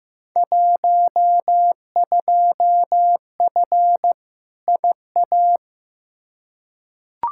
Morse code, heard as a keyed tone, about 15 wpm, 700 Hz. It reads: 12F IA
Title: Field Day Exchanges Encore - 15wpm